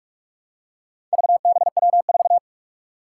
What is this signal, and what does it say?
Morse code, heard as a keyed tone, about 45 wpm, 700 Hz. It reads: VBW4
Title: Four Challenging Letter-Number Combinations 45wpm